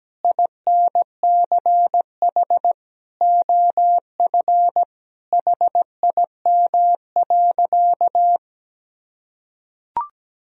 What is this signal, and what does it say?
Morse code, heard as a keyed tone, about 17 wpm, 700 Hz. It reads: INCH OF HIM.
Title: Sentences from Top 1000 Words - Review Part 3 - 17wpm